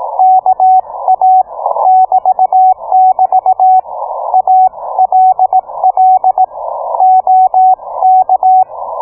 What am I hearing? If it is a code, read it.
Morse code, heard as a keyed tone, about 18 wpm, 750 Hz, with radio noise. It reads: KA <BT><BT> ALL OK